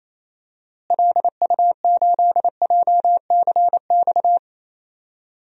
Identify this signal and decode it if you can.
Morse code, heard as a keyed tone, about 28 wpm, 700 Hz. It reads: LU8JCX
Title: DXCC-100 Call Signs - 6 Characters - Repeated 3x - 28wpm